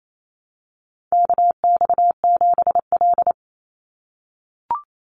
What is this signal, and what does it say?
Morse code, heard as a keyed tone, about 28 wpm, 700 Hz. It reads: KX7L
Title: NA SOTA Chaser Callsigns 28wpm